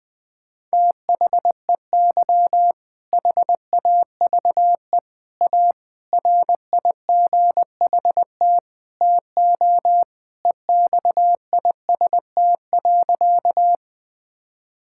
Morse code, heard as a keyed tone, about 20 wpm, 700 Hz. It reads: THEY HAVE A RIGHT TO EXIST.
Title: Sentences from Top 1000 Words - Review Part 5 - 20wpm